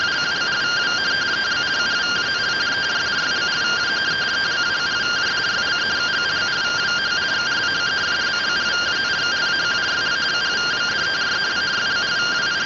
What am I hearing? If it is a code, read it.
Two-tone teleprinter signal, about 170 Hz shift.